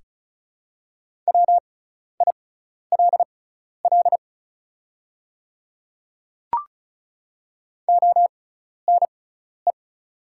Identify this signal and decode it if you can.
Morse code, heard as a keyed tone, about 34 wpm, spaced as for 12 wpm, 700 Hz. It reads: WILL ONE
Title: 6x Character Spacing - Top 200 Words - 35wpm